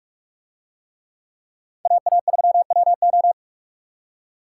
Morse code, heard as a keyed tone, about 44 wpm, 700 Hz. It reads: AA3WO